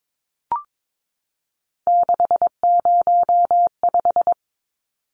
Morse code, heard as a keyed tone, about 22 wpm, 700 Hz. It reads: 605
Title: Numbers - 3-Digit 22wpm